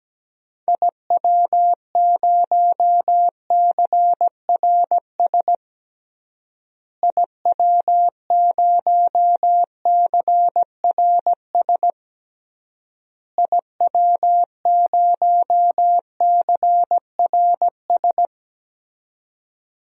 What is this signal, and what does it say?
Morse code, heard as a keyed tone, about 17 wpm, 700 Hz. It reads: IW0CRS IW0CRS IW0CRS
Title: DXCC-100 Call Signs - 6 Characters - Repeated 3x - 17wpm